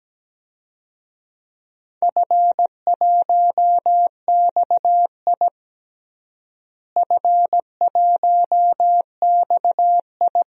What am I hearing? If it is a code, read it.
Morse code, heard as a keyed tone, about 17 wpm, 700 Hz. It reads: F1XI F1XI
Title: DXCC-100 Call Signs - 4 Characters - Repeated 3x - 17wpm